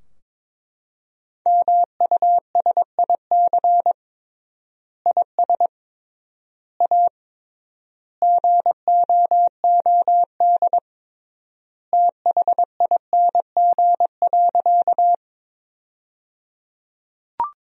Morse code, heard as a keyed tone, about 22 wpm, 700 Hz. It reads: MUSIC IS A GOOD THING.
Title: 3x Word-Spacing - Sentences from Top 300 Words - 22wpm